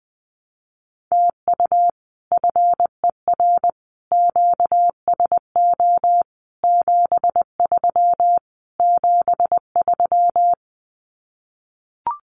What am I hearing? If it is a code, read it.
Morse code, heard as a keyed tone, about 20 wpm, 700 Hz. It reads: TU FER QSO 73 73
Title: Common QSO Phrases Encore 20wpm